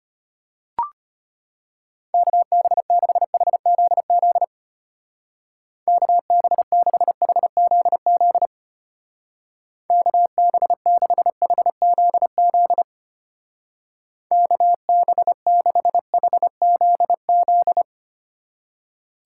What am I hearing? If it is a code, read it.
Morse code, heard as a keyed tone, about 32 wpm, 700 Hz. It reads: KB6HZZ KB6HZZ KB6HZZ KB6HZZ